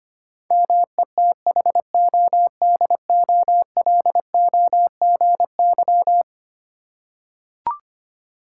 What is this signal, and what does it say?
Morse code, heard as a keyed tone, about 25 wpm, 700 Hz. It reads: METHODOLOGY
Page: morse-code-ninja-practice-25wpm